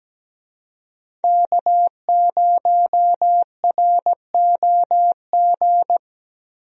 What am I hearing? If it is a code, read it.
Morse code, heard as a keyed tone, about 17 wpm, 700 Hz. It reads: K0ROG
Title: USA Call Signs - 5 Characters - Repeated 3x - 17wpm